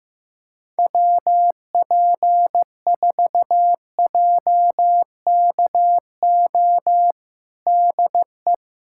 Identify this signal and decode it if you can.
Morse code, heard as a keyed tone, about 15 wpm, 700 Hz. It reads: WP4JKO DE
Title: USA Callsign DE Callsign 15wpm